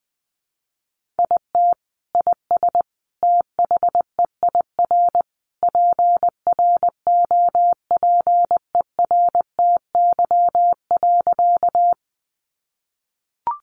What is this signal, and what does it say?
Morse code, heard as a keyed tone, about 20 wpm, 700 Hz. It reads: IT IS THEIR PROPERTY.
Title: Sentences from Top 1000 Words 20wpm